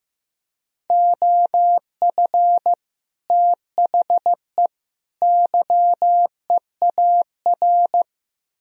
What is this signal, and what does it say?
Morse code, heard as a keyed tone, about 16 wpm, 700 Hz. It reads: OF THE YEAR
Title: Speed-Racing - Sets of 3 Words 15wpm